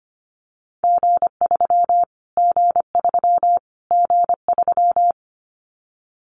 Morse code, heard as a keyed tone, about 25 wpm, 700 Hz. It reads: G3 G3 G3
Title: Callsign Prefixes for Top 25 DXCC - Repeated 3x - 25wpm